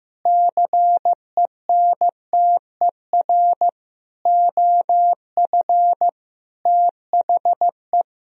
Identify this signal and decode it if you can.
Morse code, heard as a keyed tone, about 15 wpm, 700 Hz. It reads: CENTER OF THE
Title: Sentences from Top 1000 Words - Review Part 2 - 15wpm